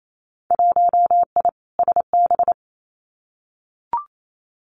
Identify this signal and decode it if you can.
Morse code, heard as a keyed tone, about 28 wpm, 700 Hz. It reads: 1I SB
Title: Winter Field Day Exchanges 28wpm